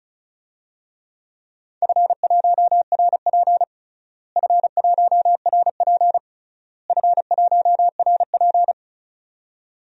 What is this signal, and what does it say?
Morse code, heard as a keyed tone, about 36 wpm, 700 Hz. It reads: F1RP F1RP F1RP
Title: DXCC-100 Call Signs - 4 Characters - Repeated 3x - 35wpm